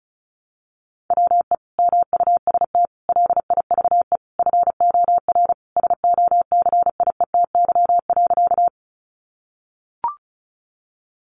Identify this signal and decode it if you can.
Morse code, heard as a keyed tone, about 34 wpm, 700 Hz. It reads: WE MUST LIVE FOR SOCIETY.
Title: Sentences from Top 700 Words 35wpm